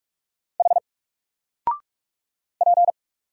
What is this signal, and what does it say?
Morse code, heard as a keyed tone, about 44 wpm, 700 Hz. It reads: H P
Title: Single Letter-Number Alt 45wpm